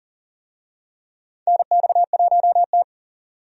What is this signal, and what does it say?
Morse code, heard as a keyed tone, about 40 wpm, 700 Hz. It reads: NX1T
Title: USA Call Signs - 4 Characters - 40wpm